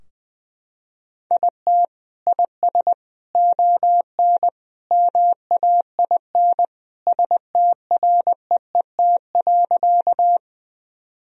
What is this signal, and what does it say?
Morse code, heard as a keyed tone, about 20 wpm, 700 Hz. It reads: IT IS ON MAIN STREET.
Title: Sentences from Top 1000 Words - Review Part 2 - 20wpm